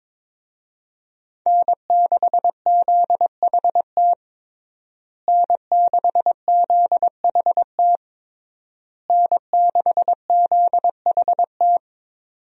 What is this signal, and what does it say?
Morse code, heard as a keyed tone, about 22 wpm, 700 Hz. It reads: N6ZHT N6ZHT N6ZHT
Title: USA Call Signs - 5 Characters - Repeated 3x - 22wpm